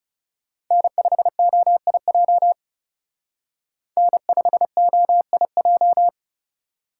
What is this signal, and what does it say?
Morse code, heard as a keyed tone, about 36 wpm, 700 Hz. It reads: N5OIJ N5OIJ
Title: Speed Racing - USA 1x3 Call Signs 30wpm